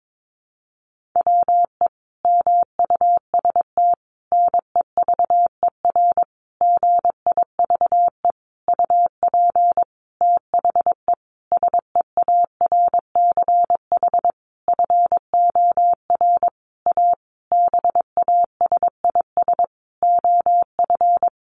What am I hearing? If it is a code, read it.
Morse code, heard as a keyed tone, about 22 wpm, 700 Hz. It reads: WE MUST NEVER GIVE UP THE SEARCH FOR A BASIS OF
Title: Sentences from Top 1500 Words 22wpm